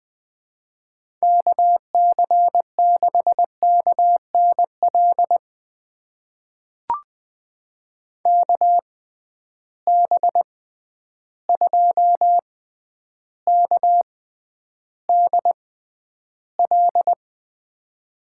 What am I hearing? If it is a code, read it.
Morse code, heard as a keyed tone, about 20 wpm, spaced as for 10 wpm, 700 Hz. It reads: KC6KNL KB2KDL
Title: USA Call Signs - 6 Characters - 6x Character Spacing - 20wpm